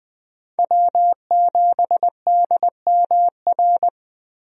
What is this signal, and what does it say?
Morse code, heard as a keyed tone, about 20 wpm, 700 Hz. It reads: W7DMR
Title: USA Call Signs - 5 Characters - Repeated 3x - 20wpm